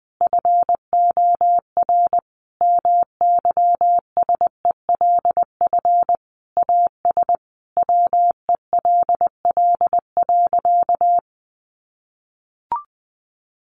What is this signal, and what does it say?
Morse code, heard as a keyed tone, about 20 wpm, 700 Hz. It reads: FOR MYSELF AS WELL.
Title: Sentences from Top 1300 Words 20wpm